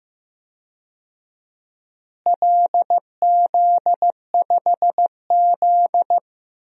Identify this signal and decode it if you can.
Morse code, heard as a keyed tone, about 15 wpm, 700 Hz. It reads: LZ5Z
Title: DXCC-100 Call Signs - 4 Characters - Repeated 3x - 15wpm